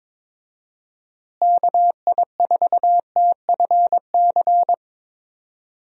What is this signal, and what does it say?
Morse code, heard as a keyed tone, about 21 wpm, 700 Hz. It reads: KI4TFC
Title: USA Call Signs - 6 Characters - Repeated 3x - 22wpm